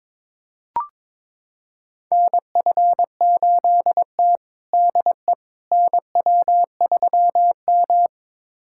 Morse code, heard as a keyed tone, about 22 wpm, 700 Hz. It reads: NF8T DE NW3M
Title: USA Callsign DE Callsign - 4 Characters - 22wpm